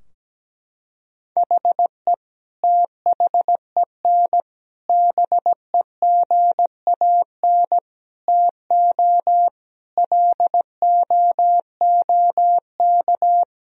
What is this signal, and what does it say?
Morse code, heard as a keyed tone, about 17 wpm, 700 Hz. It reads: HE THEN BEGAN TO LOOK
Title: Sentences from Top 1000 Words - Review Part 3 - 17wpm